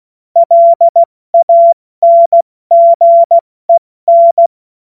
Morse code, heard as a keyed tone, about 16 wpm, 680 Hz. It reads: LANGEN
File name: Morsekurs / Witze / Witz 19 Dezember 2023 70 F80